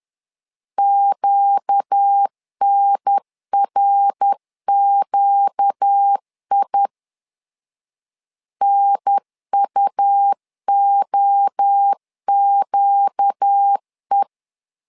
Morse code, heard as a keyed tone, about 10 wpm, 800 Hz. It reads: QNRQI NUOQE
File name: LEZIONI / 4 Q 10wpm 2210